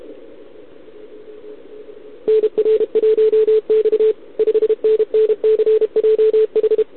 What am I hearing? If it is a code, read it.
Morse code, heard as a keyed tone, about 32 wpm, 440 Hz. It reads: NR1X 5NNCJH
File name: ARCHIV / OKDX2020 / OK1RIb